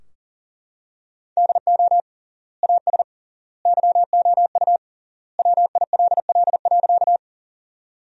Morse code, heard as a keyed tone, about 40 wpm, 700 Hz. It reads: DO AS YOU WILL.
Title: 3x Word-Spacing - Sentences from Top 100 Words - 40wpm